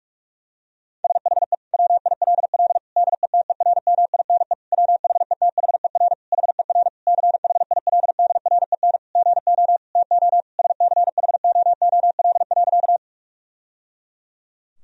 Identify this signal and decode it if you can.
Morse code, heard as a keyed tone, about 46 wpm, 700 Hz. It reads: SHE WILL DETERMINE WHETHER HER CHILDREN GO TO SCHOOL.
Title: Sentences from Top 1000 Words - Review Part 4 - 45wpm